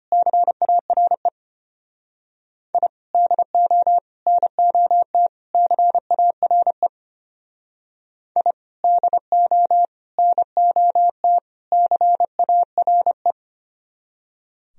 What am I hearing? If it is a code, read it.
Morse code, heard as a keyed tone, about 31 wpm, 700 Hz. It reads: CARE I DO NOT CARE I DO NOT CARE
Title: Speed-Racing - Sets of 4 Words 25wpm